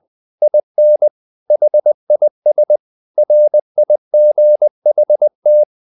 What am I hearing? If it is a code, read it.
Morse code, heard as a keyed tone, about 20 wpm, 600 Hz. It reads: IN HIS RIGHT
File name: the old man and the sea part 2 20wpm